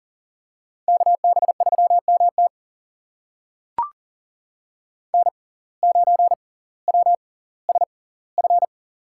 Morse code, heard as a keyed tone, about 40 wpm, spaced as for 21 wpm, 690 Hz. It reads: KB3MT N9WSF
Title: USA Call Signs - 5 Characters - 6x Character Spacing - 40wpm